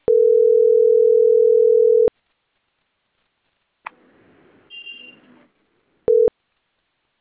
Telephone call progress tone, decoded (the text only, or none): ringback tone